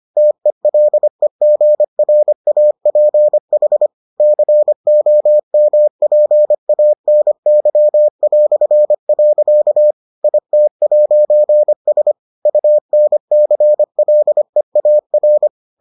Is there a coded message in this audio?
Yes: Morse code from 0.2 to 15.5 s, heard as a keyed tone, about 25 wpm, 600 Hz. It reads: TELEGRAPH COMPANY". IT'S UNCLEAR